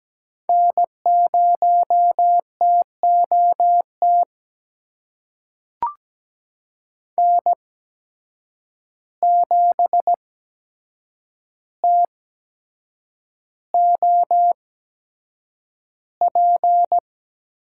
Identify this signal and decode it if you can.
Morse code, heard as a keyed tone, about 17 wpm, spaced as for 7 wpm, 700 Hz. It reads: N0TOT N7TOP
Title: USA Call Signs - 5 Characters - 8x Character Spacing - 17wpm